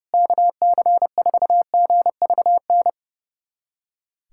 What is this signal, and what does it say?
Morse code, heard as a keyed tone, about 30 wpm, 700 Hz. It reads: KC4GVN
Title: USA Callsign DE Callsign - 6 Characters - 30wpm